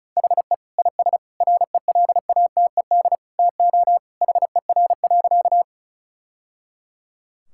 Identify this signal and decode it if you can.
Morse code, heard as a keyed tone, about 36 wpm, 700 Hz. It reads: HE IS RELATED TO HER.
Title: Sentences from Top 1000 Words - Review Part 5 - 35wpm